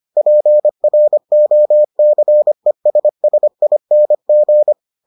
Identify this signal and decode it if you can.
Morse code, heard as a keyed tone, about 25 wpm, 600 Hz. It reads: PROCESSING